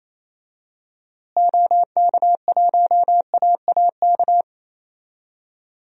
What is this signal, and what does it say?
Morse code, heard as a keyed tone, about 28 wpm, 700 Hz. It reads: OK1AAK
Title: DXCC-100 Call Signs - 6 Characters - Repeated 2x - 28wpm